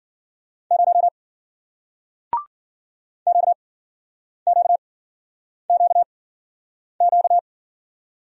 Morse code, heard as a keyed tone, about 55 wpm, spaced as for 9 wpm, 700 Hz. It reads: 0 QQQQ